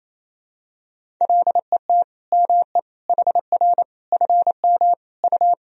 Morse code, heard as a keyed tone, about 28 wpm, 700 Hz. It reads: LET ME HR FM U
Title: Sentences from Top 400 Words and Abbreviations 28wpm